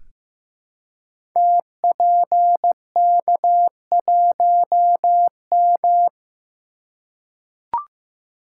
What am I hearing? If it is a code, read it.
Morse code, heard as a keyed tone, about 15 wpm, 700 Hz. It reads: TPK1M
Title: Code Groups - 5 Letter:Numbers - 15wpm